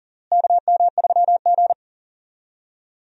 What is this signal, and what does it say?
Morse code, heard as a keyed tone, about 40 wpm, 700 Hz. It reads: KM3G
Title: USA Call Signs - 4 Characters - Repeated 2x - 40wpm